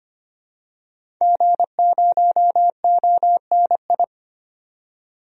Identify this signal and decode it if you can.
Morse code, heard as a keyed tone, about 25 wpm, 700 Hz. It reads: G0ONI
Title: DXCC-100 Call Signs - 5 Characters - 25wpm